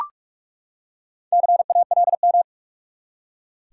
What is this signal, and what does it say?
Morse code, heard as a keyed tone, about 43 wpm, 700 Hz. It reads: CALM
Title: Four Letter Words - 45wpm